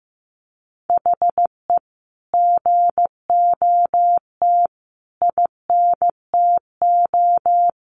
Morse code, heard as a keyed tone, about 15 wpm, 700 Hz. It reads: HE GOT INTO
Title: Sentences from Top 1300 Words 15wpm